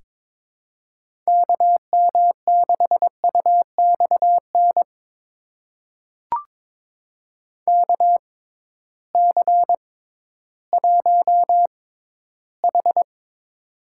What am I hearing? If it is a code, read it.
Morse code, heard as a keyed tone, about 22 wpm, spaced as for 13 wpm, 700 Hz. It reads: KM6UXN KC1H
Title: USA Call Signs - 6x Character Spacing - 22wpm